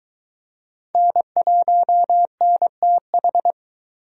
Morse code, heard as a keyed tone, about 23 wpm, 700 Hz. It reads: N1NTH